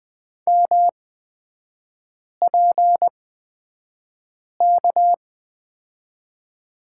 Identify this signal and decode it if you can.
Morse code, heard as a keyed tone, about 20 wpm, spaced as for 5 wpm, 700 Hz. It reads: MPK
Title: 243 Lesson 39e - Letter z - Three Characters 20wpm